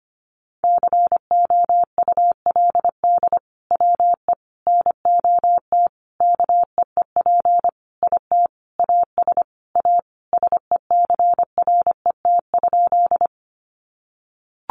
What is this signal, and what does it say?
Morse code, heard as a keyed tone, about 25 wpm, 700 Hz. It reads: COULD WE NOT KEEP IT AS A SECRET?
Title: Sentences from Top 1200 Words 25wpm